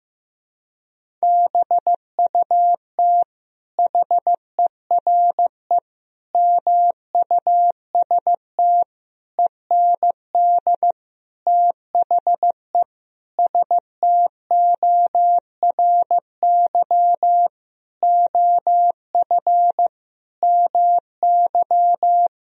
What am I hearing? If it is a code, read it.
Morse code, heard as a keyed tone, about 15 wpm, 700 Hz. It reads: BUT HERE MUST END THE STORY OF MY
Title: Sentences from Top 1000 Words - Review Part 1 - 15wpm